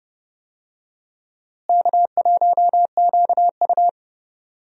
Morse code, heard as a keyed tone, about 32 wpm, 700 Hz. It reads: K1QU
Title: USA Call Signs - 4 Characters - Repeated 3x - 30wpm